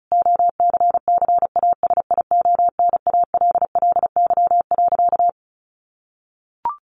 Morse code, heard as a keyed tone, about 34 wpm, 700 Hz. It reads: OCCASIONALLY.
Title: Sentences from Top 1300 Words 35wpm